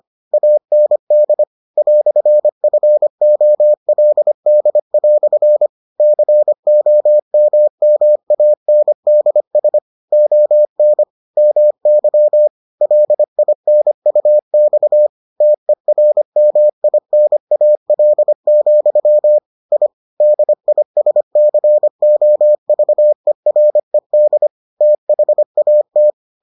Morse code, heard as a keyed tone, about 25 wpm, 600 Hz. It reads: AND "FOLD" COMMANDS ON MY LINUX TERMINAL, I DISCOVERED THAT